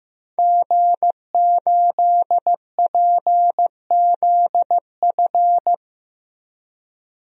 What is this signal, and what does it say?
Morse code, heard as a keyed tone, about 16 wpm, 700 Hz. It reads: G8PZF
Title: DXCC-100 Call Signs - 5 Characters - 15wpm